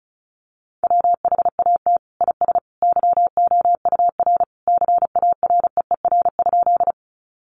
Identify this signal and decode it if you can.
Morse code, heard as a keyed tone, about 34 wpm, 700 Hz. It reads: WHAT IS YOUR CAREER?
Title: Sentences from Top 1000 Words 35wpm